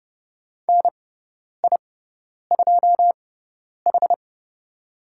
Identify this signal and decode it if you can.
Morse code, heard as a keyed tone, about 30 wpm, spaced as for 10 wpm, 700 Hz. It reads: NI2H
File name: Lesson 16g - Letter c - Call Signs 30wpm